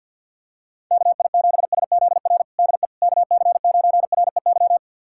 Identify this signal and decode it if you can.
Morse code, heard as a keyed tone, about 49 wpm, 700 Hz. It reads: KI7SZR DE KC9REY
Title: USA Callsign DE Callsign - 6 Characters - 50wpm